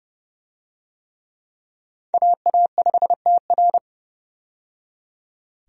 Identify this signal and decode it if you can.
Morse code, heard as a keyed tone, about 30 wpm, 700 Hz. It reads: AA5TR
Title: USA Call Signs - 5 Characters - 30wpm